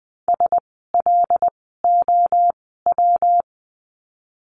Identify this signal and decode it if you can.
Morse code, heard as a keyed tone, about 20 wpm, spaced as for 15 wpm, 700 Hz. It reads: SLOW
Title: 115 Lesson 19f - Letter w - Words 20wpm